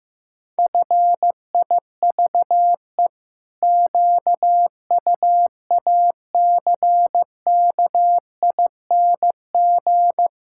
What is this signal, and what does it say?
Morse code, heard as a keyed tone, about 16 wpm, 700 Hz. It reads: FIVE QUACKING
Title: Pangram Challenge v2 - 15wpm